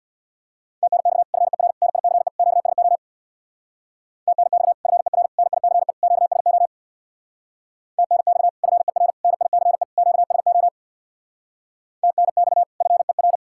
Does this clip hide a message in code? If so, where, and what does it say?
Yes: Morse code from 0.8 to 13.4 s, heard as a keyed tone, about 66 wpm, 700 Hz. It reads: TNX FER NICE QSO TNX FER NICE QSO TNX FER NICE QSO TNX FER